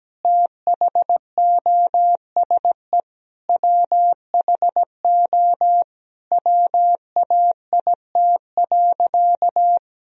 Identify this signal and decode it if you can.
Morse code, heard as a keyed tone, about 17 wpm, 700 Hz. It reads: THOSE WHO WAIT.
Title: Sentences from Top 1000 Words - Review Part 2 - 17wpm